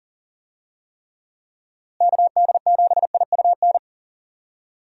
Morse code, heard as a keyed tone, about 40 wpm, 700 Hz. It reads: KD7IUN
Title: USA Call Signs - 6 Characters - Repeated 3x - 40wpm